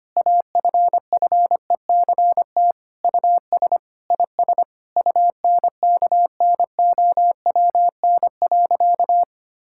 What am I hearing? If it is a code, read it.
Morse code, heard as a keyed tone, about 25 wpm, 700 Hz. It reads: AFFECT US IS UNKNOWN.
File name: practice / 25wpm / Sentences with 4-Word Phrases - 25wpm